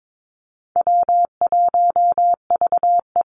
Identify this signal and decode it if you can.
Morse code, heard as a keyed tone, about 22 wpm, 700 Hz. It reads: W1VE
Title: Easy Call Signs - 22wpm